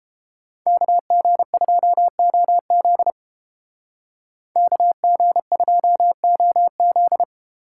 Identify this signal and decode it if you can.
Morse code, heard as a keyed tone, about 30 wpm, 700 Hz. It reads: KG2OZ KG2OZ